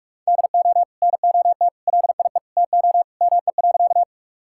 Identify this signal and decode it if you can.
Morse code, heard as a keyed tone, about 46 wpm, 700 Hz. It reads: DO NOT LIE TO ME.
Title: Sentences from Top 1600 Words 45wpm